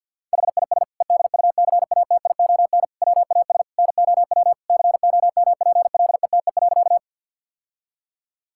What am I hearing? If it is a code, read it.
Morse code, heard as a keyed tone, about 50 wpm, 700 Hz. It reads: HIS EDUCATION WAS NOW COMPLETE.
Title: Sentences from Top 1000 Words - Review Part 6 - 50wpm